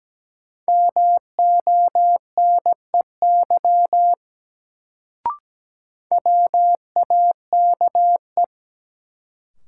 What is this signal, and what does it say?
Morse code, heard as a keyed tone, about 17 wpm, 700 Hz. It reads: MONEY WAKE
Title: Sets of 4 Words - New Word Drill - 17wpm